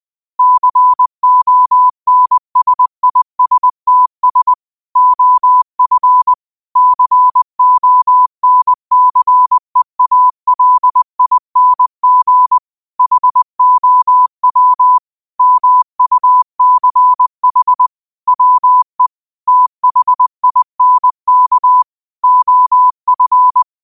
Morse code, heard as a keyed tone, about 20 wpm, 1000 Hz. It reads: CONSISTS OF CONCEALING HOW MUCH WE THINK OF
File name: 20wpm